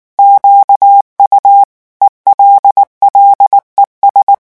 Morse code, heard as a keyed tone, about 19 wpm, 800 Hz. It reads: QU ELLES
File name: morse mp3 / revision9 citations humoristiques vitesse 18 mots minute